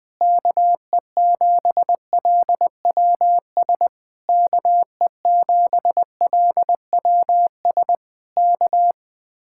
Morse code, heard as a keyed tone, about 20 wpm, 700 Hz. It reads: KE7LWS KE7LWS K
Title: Speed-Racing Common QSO Phrases Encore - 17wpm